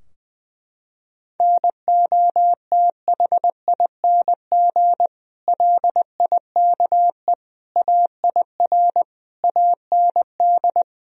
Morse code, heard as a keyed tone, about 20 wpm, 700 Hz. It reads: NOTHING LIKE AIR AND
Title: Sentences from Top 1000 Words - Review Part 4 - 20wpm